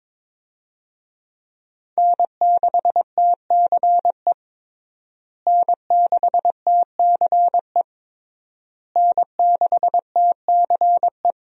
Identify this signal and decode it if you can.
Morse code, heard as a keyed tone, about 22 wpm, 700 Hz. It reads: N6TCE N6TCE N6TCE
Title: USA Call Signs - 5 Characters - Repeated 3x - 22wpm